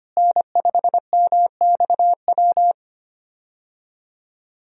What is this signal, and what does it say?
Morse code, heard as a keyed tone, about 25 wpm, 700 Hz. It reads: N5MXW
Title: USA Call Signs - 5 Characters - Repeated 3x - 25wpm